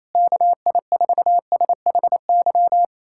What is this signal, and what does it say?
Morse code, heard as a keyed tone, about 28 wpm, 700 Hz. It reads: KI4SHY
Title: USA Callsign DE Callsign - 6 Characters - 28wpm